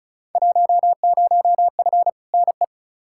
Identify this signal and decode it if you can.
Morse code, heard as a keyed tone, about 36 wpm, 700 Hz. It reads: 10F NE
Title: Field Day Exchanges Encore - 35wpm